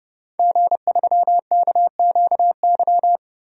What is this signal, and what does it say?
Morse code, heard as a keyed tone, about 30 wpm, 700 Hz. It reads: G3KQY
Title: DXCC-100 Call Signs - 5 Characters - Repeated 3x - 30wpm